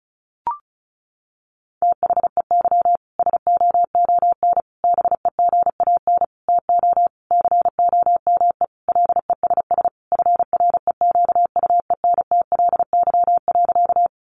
Morse code, heard as a keyed tone, about 34 wpm, 700 Hz. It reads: THEY SOON BEGAN TO COME LESS FREQUENTLY.
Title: Sentences from Top 1300 Words 35wpm